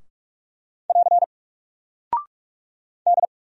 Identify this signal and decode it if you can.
Morse code, heard as a keyed tone, about 44 wpm, 700 Hz. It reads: <AR> D
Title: Single Character (Letter:Number:Punctuation:Prosign) 45wpm